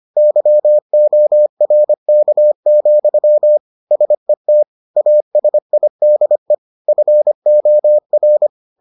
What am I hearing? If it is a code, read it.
Morse code, heard as a keyed tone, about 25 wpm, 600 Hz. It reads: YORK, SET ASIDE FOR